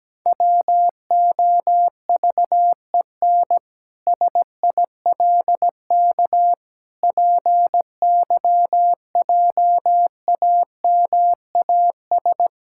Morse code, heard as a keyed tone, about 17 wpm, 700 Hz. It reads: WOVEN SILK PYJAMAS
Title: Pangram Challenge v2 - 17wpm